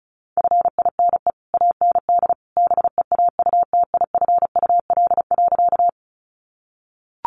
Morse code, heard as a keyed tone, about 34 wpm, 700 Hz. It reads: FINE AND BEAUTIFUL.
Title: Sentences from Top 900 Words 35wpm